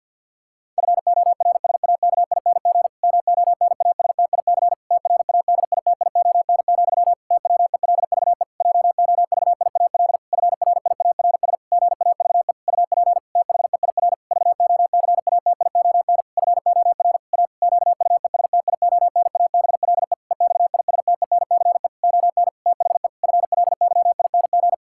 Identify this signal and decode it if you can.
Morse code, heard as a keyed tone, about 49 wpm, 700 Hz. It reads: FORSAKING MONASTIC TRADITION, TWELVE JOVIAL FRIARS GAVE UP THEIR VOCATION FOR A QUESTIONABLE EXISTENCE ON THE FLYING
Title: Pangram Challenge v2 - 50wpm